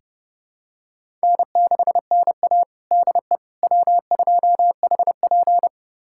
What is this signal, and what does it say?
Morse code, heard as a keyed tone, about 30 wpm, 700 Hz. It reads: N6NA DE W2HP
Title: USA Callsign DE Callsign - 4 Characters - 30wpm